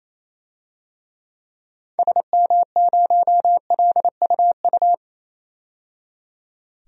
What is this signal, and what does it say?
Morse code, heard as a keyed tone, about 28 wpm, 700 Hz. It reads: SM0LUU
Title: DXCC-100 Call Signs - 6 Characters - 28wpm